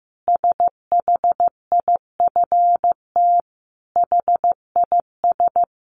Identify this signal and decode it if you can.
Morse code, heard as a keyed tone, about 15 wpm, 700 Hz. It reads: SHIFT HIS
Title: A Sound Of Thunder 15wpm